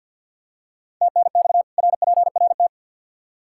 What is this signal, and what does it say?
Morse code, heard as a keyed tone, about 53 wpm, 700 Hz. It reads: TNX RPRT
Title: K1USN SST QSO Practice 50wpm